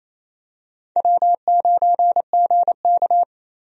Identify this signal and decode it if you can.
Morse code, heard as a keyed tone, about 28 wpm, 700 Hz. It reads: W9GK
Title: USA Call Signs - 4 Characters - Repeated 3x - 28wpm